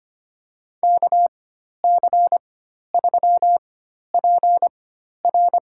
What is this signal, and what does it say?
Morse code, heard as a keyed tone, about 25 wpm, spaced as for 12 wpm, 700 Hz. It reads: KC3PR
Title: USA Call Signs - 5 Characters - 4x Character Spacing - 25wpm